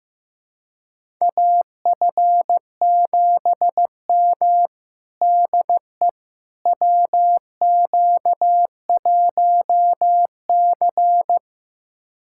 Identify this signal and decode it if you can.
Morse code, heard as a keyed tone, about 15 wpm, 700 Hz. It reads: AF7M DE WQ1C
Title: USA Callsign DE Callsign - 4 Characters - 15wpm